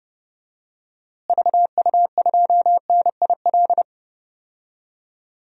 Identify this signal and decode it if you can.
Morse code, heard as a keyed tone, about 30 wpm, 700 Hz. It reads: VU2NIL
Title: DXCC-100 Call Signs - 6 Characters - Repeated 2x - 30wpm